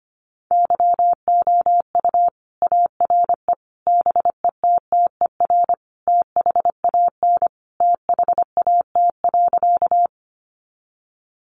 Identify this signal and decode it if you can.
Morse code, heard as a keyed tone, about 25 wpm, 700 Hz. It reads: YOU ARE BETTER THAN THAT.
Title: Sentences from Top 400 Words 25wpm